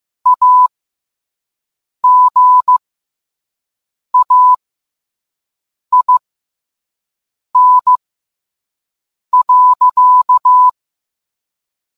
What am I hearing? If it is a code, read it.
Morse code, heard as a keyed tone, about 15 wpm, spaced as for 5 wpm, 1000 Hz. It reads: AGAIN.